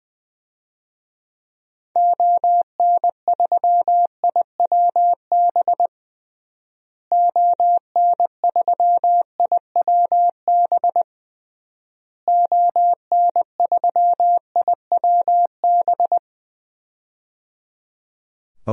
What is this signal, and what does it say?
Morse code, heard as a keyed tone, about 20 wpm, 700 Hz. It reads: ON3IWB ON3IWB ON3IWB
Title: DXCC-100 Call Signs - 6 Characters - Repeated 3x - 20wpm